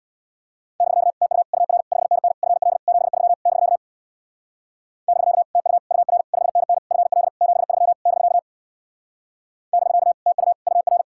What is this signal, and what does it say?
Morse code, heard as a keyed tone, about 62 wpm, 700 Hz. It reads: <BK> TU UR 5NN FL 73 <BK> <BK> TU UR 5NN FL 73 <BK> <BK> TU UR
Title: Speed-Racing Common QSO Phrases - 50wpm